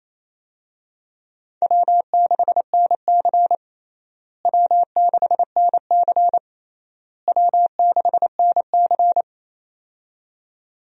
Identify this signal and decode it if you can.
Morse code, heard as a keyed tone, about 28 wpm, 700 Hz. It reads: W6NC W6NC W6NC
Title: USA Call Signs - 4 Characters - Repeated 3x - 28wpm